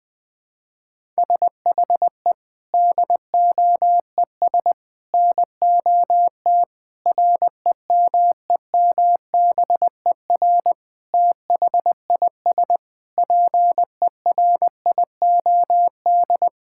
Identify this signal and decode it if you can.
Morse code, heard as a keyed tone, about 19 wpm, 700 Hz. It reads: SHE DOES NOT REMEMBER THIS PERIOD
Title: Sentences from Top 1000 Words - Review Part 5 - 20wpm